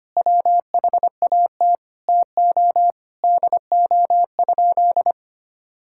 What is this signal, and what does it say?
Morse code, heard as a keyed tone, about 25 wpm, 700 Hz. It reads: WHAT TO DO?
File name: practice / 25wpm / Sentences with 2-Word Phrases - 25wpm